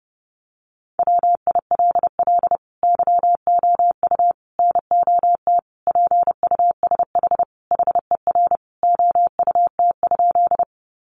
Morse code, heard as a keyed tone, about 28 wpm, 700 Hz. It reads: WILL YOU NOT PUSH HER OUT?
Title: Sentences from Top 1300 Words 30wpm